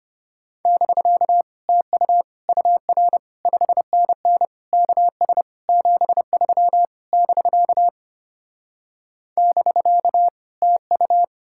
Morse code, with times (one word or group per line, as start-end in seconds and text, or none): 0.7-1.4 <BK>
1.7-2.2 TU
2.5-3.2 UR
3.5-4.5 5NN
4.7-5.4 KS
5.7-6.8 73
7.1-7.9 <BK>
9.4-10.3 <BK>
10.6-11.2 TU